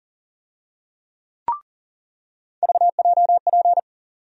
Morse code, heard as a keyed tone, about 40 wpm, 700 Hz. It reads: VJP
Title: USA Call Sign Postfixes - 3 Letters 40wpm